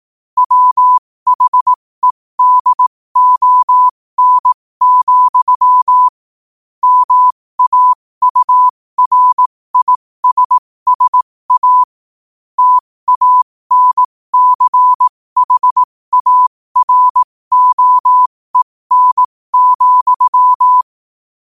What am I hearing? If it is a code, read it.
Morse code, heard as a keyed tone, about 18 wpm, spaced as for 15 wpm, 1000 Hz. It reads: WHEDON, MAURISSA TANCHAROEN,
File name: qotdfiles / 15wpm